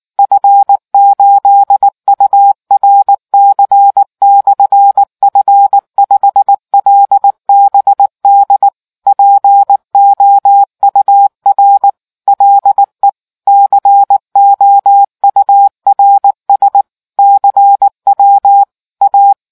Morse code, heard as a keyed tone, about 19 wpm, 800 Hz. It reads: F8URC/F5LBD POUR LE COURS CW A